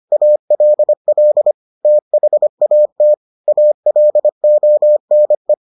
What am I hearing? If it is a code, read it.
Morse code, heard as a keyed tone, about 25 wpm, 600 Hz. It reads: ALL THAT ALONE